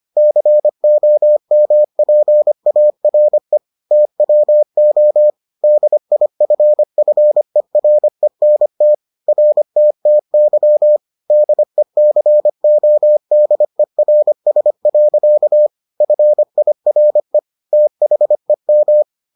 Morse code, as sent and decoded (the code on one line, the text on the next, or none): -.-. --- -- .--. .- .-. / - .-- --- / -.. .. ..-. ..-. .-. -. - / .-. - - -.-- / -.. -.-. --- -.. .-. ... .-.-.- / ..-. .. .-. / - .... --
COMPARE TWO DIFFERENT RTTY DECODERS. FIRE THEM